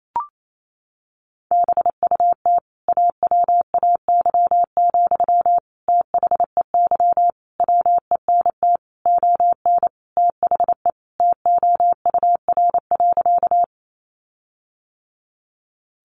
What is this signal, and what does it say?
Morse code, heard as a keyed tone, about 28 wpm, 700 Hz. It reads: BUT AWAY, THEY WENT ON THE TOUR.
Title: Sentences from Top 1400 Words 28wpm